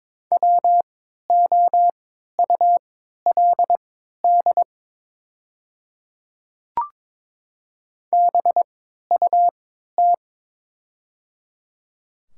Morse code, heard as a keyed tone, about 22 wpm, spaced as for 12 wpm, 690 Hz. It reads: WOULD BUT
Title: 3x Character Spacing - Top 100 Words - 22wpm